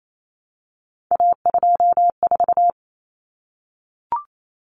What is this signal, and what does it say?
Morse code, heard as a keyed tone, about 28 wpm, 700 Hz. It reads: A24
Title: Common Cut Numbers - 3-Digit 28wpm